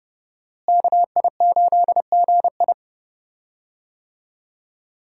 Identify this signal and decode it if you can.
Morse code, heard as a keyed tone, about 30 wpm, 700 Hz. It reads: KI8GI
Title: USA Call Signs - 5 Characters - Repeated 2x - 30wpm